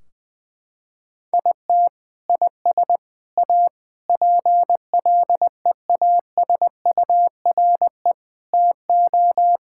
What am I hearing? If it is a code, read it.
Morse code, heard as a keyed tone, about 21 wpm, 700 Hz. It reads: IT IS A PLEASURE TO
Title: Sentences from Top 1600 Words 20wpm